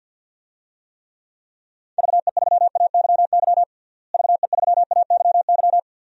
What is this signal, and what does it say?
Morse code, heard as a keyed tone, about 49 wpm, 700 Hz. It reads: VE3AYY VE3AYY
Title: DXCC-100 Call Signs - 6 Characters - Repeated 3x - 50wpm